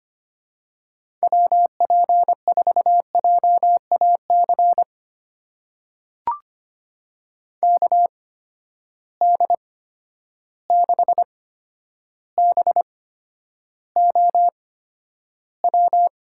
Morse code, heard as a keyed tone, about 25 wpm, spaced as for 11 wpm, 700 Hz. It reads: WP4JAC KD6BOW
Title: USA Call Signs - 6 Characters - 8x Character Spacing - 25wpm